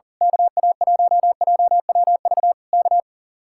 Morse code, heard as a keyed tone, about 38 wpm, 700 Hz. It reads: KA1JWV K
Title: Speed-Racing Common QSO Phrases - 40wpm